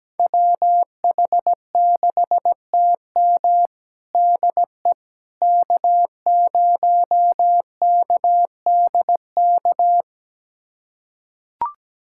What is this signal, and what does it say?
Morse code, heard as a keyed tone, about 17 wpm, 700 Hz. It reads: WH6TM DE K0KDK
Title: USA Callsign DE Callsign - 5 Characters - 17wpm